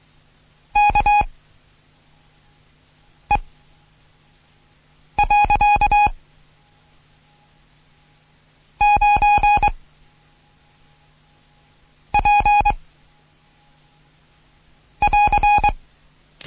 Morse code, heard as a keyed tone, about 23 wpm, 830 Hz. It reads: K E . 9 P <AR>
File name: Morsen / Kurs A 16